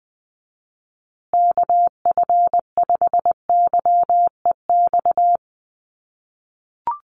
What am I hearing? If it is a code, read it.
Morse code, heard as a keyed tone, about 19 wpm, 700 Hz. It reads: KF5YEX
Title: USA Call Signs Repeated 3x 20wpm